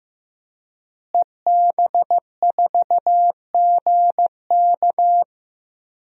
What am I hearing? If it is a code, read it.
Morse code, heard as a keyed tone, about 15 wpm, 700 Hz. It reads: EB4GK
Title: DXCC-100 Call Signs - 5 Characters - Repeated 2x - 15wpm